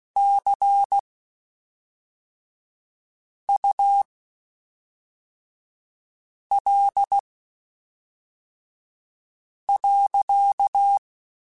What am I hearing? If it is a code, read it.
Morse code, heard as a keyed tone, about 16 wpm, spaced as for 3 wpm, 800 Hz. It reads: CUL.
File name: (spaced) / QSOs Spaced-mp3s / kc7sac @5 sp